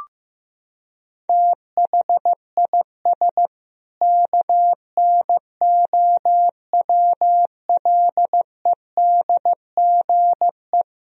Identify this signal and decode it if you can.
Morse code, heard as a keyed tone, about 15 wpm, 700 Hz. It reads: THIS KNOWLEDGE
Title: Sentences from Top 1000 Words - Review Part 3 - 15wpm